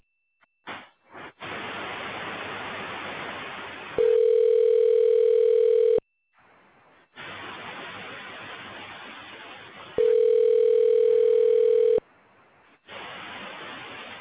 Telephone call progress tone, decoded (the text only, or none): ringback tone